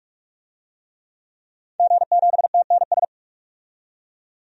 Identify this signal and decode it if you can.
Morse code, heard as a keyed tone, about 46 wpm, 700 Hz. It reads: G7TNS
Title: DXCC-100 Call Signs - 5 Characters - 45wpm